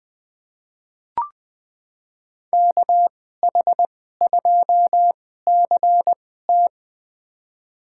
Morse code, heard as a keyed tone, about 20 wpm, spaced as for 14 wpm, 700 Hz. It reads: KH2CT
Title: USA Call Signs - 5 Characters - 2x Character Spacing - 20wpm